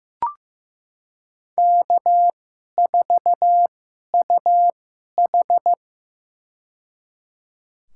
Morse code, heard as a keyed tone, about 15 wpm, spaced as for 11 wpm, 700 Hz. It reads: K4UH
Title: USA Call Signs - 4 Characters - 2x Character Spacing - 15wpm